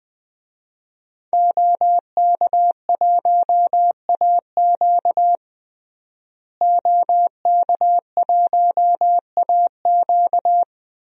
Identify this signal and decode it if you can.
Morse code, heard as a keyed tone, about 20 wpm, 700 Hz. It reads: OK1AQ OK1AQ
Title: DXCC-100 Call Signs - 5 Characters - Repeated 2x - 20wpm